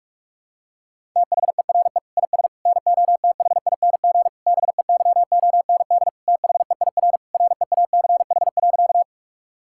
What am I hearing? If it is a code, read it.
Morse code, heard as a keyed tone, about 42 wpm, 700 Hz. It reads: THERE IS NOTHING BEYOND THEIR REACH.
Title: Sentences from Top 1000 Words - Review Part 6 - 45wpm